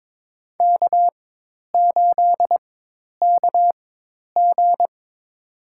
Morse code, heard as a keyed tone, about 21 wpm, spaced as for 10 wpm, 700 Hz. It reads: K8KG
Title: USA Call Signs - 4 Characters - 4x Character Spacing - 22wpm